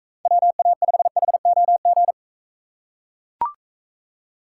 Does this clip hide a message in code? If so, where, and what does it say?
Yes: Morse code from 0.3 to 2.1 s, heard as a keyed tone, about 41 wpm, 700 Hz. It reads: WA5HOG